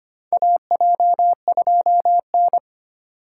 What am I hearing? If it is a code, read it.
Morse code, heard as a keyed tone, about 25 wpm, 700 Hz. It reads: AJ2N
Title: USA Call Signs - 2x Character Spacing - 25wpm